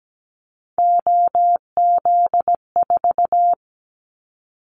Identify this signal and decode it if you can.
Morse code, heard as a keyed tone, about 17 wpm, 700 Hz. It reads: OZ4
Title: Callsign Prefixes for Top 50 DXCC - 17wpm